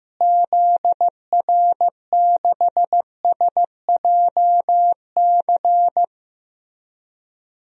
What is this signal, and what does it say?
Morse code, heard as a keyed tone, about 15 wpm, 700 Hz. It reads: ZR6SJC
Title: DXCC-100 Call Signs - 6 Characters - Repeated 3x - 15wpm